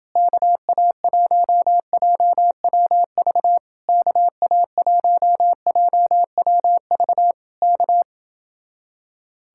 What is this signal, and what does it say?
Morse code, heard as a keyed tone, about 31 wpm, 700 Hz. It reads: KA1JWV KA1JWV K